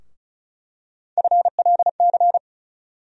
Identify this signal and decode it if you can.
Morse code, heard as a keyed tone, about 35 wpm, 700 Hz. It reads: FLC
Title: USA Call Sign Postfixes - 3 Letters 35wpm